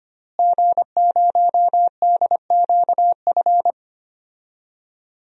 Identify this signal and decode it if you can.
Morse code, heard as a keyed tone, about 25 wpm, 700 Hz. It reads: G0DQF
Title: DXCC-100 Call Signs - 5 Characters - Repeated 3x - 25wpm